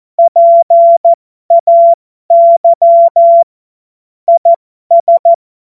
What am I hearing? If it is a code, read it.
Morse code, heard as a keyed tone, about 14 wpm, spaced as for 12 wpm, 680 Hz. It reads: PAY IS
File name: Morsekurs / Witze